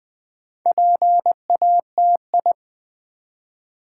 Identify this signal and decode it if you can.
Morse code, heard as a keyed tone, about 20 wpm, 700 Hz. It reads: PATI
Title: Build A Word - 2 Letters At A Time - 8-Letter Words - 20wpm